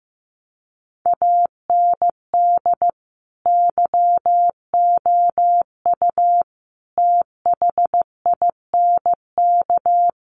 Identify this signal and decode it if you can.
Morse code, heard as a keyed tone, about 15 wpm, 700 Hz. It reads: AND YOU THINK
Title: Sentences from Top 1300 Words 15wpm